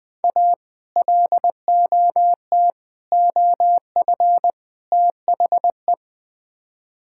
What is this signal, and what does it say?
Morse code, heard as a keyed tone, about 20 wpm, 700 Hz. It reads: A LOT OF THE
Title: Build A Phrase - 4-Word Phrases - 20wpm